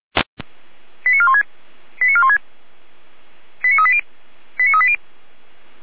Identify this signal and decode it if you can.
Five-tone selective call (ZVEI), heard as quick runs of five tones, tones of about 70 ms. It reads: 89416 89416 89380 89380